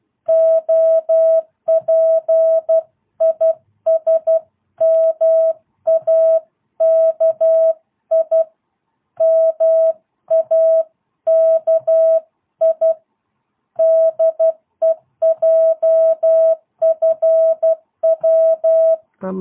Morse code, heard as a keyed tone, about 12 wpm, 660 Hz. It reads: OP IS MAKI MAKI DE JFW